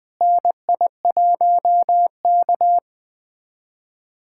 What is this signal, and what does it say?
Morse code, heard as a keyed tone, about 20 wpm, 700 Hz. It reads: NI1K
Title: USA Callsign DE Callsign - 4 Characters - 20wpm